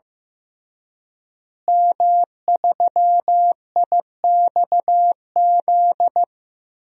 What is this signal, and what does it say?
Morse code, heard as a keyed tone, about 15 wpm, 700 Hz. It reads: M3IXZ
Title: DXCC-100 Call Signs - 5 Characters - Repeated 3x - 15wpm